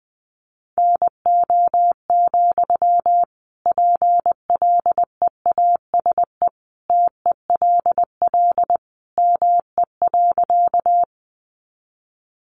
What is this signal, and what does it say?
Morse code, heard as a keyed tone, about 20 wpm, 700 Hz. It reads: NO, PLEASE TELL ME.
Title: Sentences from Top 1500 Words 20wpm